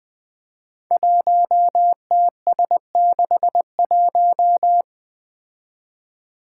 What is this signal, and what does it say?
Morse code, heard as a keyed tone, about 20 wpm, 700 Hz. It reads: 1TS61
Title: Five Challenging Letter-Number Combinations 20wpm